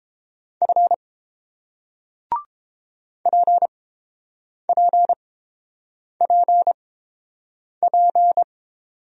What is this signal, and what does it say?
Morse code, heard as a keyed tone, about 30 wpm, spaced as for 7 wpm, 700 Hz. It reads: F PPPP